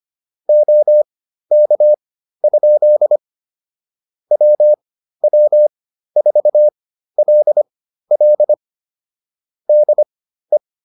Morse code, heard as a keyed tone, about 25 wpm, spaced as for 13 wpm, 600 Hz. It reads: OK? WW4LL DE